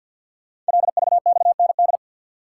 Morse code, heard as a keyed tone, about 53 wpm, 700 Hz. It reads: LVXND
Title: Code Groups - 5 Letters - 50wpm